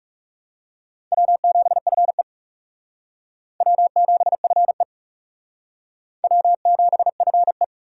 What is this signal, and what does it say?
Morse code, heard as a keyed tone, about 35 wpm, 700 Hz. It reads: W7FE W7FE W7FE
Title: Speed Racing - USA 1x2 Call Signs 30wpm